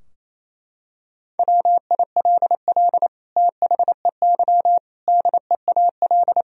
Morse code, heard as a keyed tone, about 28 wpm, 700 Hz. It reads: WILL THEY DEAL
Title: Sentences from Top 1000 Words - Review Part 4 - 28wpm